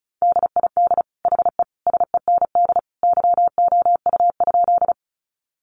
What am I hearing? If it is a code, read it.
Morse code, heard as a keyed tone, about 34 wpm, 700 Hz. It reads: DID HE SEND YOU?
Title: Sentences from Top 1500 Words 35wpm